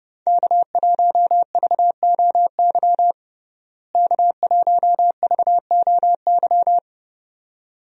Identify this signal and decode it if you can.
Morse code, heard as a keyed tone, about 30 wpm, 700 Hz. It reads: K1VOY K1VOY
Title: USA Call Signs - 5 Characters - Repeated 2x - 30wpm